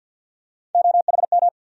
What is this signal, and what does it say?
Morse code, heard as a keyed tone, about 50 wpm, 700 Hz. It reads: OHM
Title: USA Call Sign Postfixes - 3 Letters 50wpm